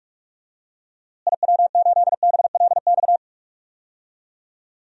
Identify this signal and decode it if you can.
Morse code, heard as a keyed tone, about 42 wpm, 700 Hz. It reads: IW8BLX
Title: DXCC-100 Call Signs - 6 Characters - Repeated 3x - 45wpm